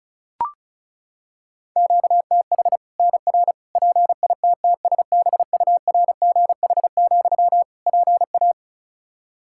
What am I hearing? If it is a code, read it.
Morse code, heard as a keyed tone, about 36 wpm, 700 Hz. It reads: QTH NR PITTSBURGH, PA
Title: Practice QSO - Exchange 1 - Beginning - 35wpm